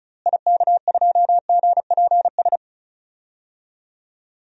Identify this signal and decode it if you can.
Morse code, heard as a keyed tone, about 36 wpm, 700 Hz. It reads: IK2GPS
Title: DXCC-100 Call Signs - 6 Characters - Repeated 2x - 35wpm